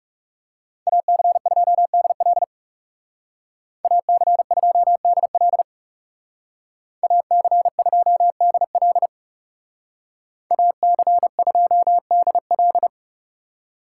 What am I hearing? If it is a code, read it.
Morse code, heard as a keyed tone, about 36 wpm, 700 Hz. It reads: AC2DL AC2DL AC2DL AC2DL
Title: Speed Racing - USA Call Signs 30wpm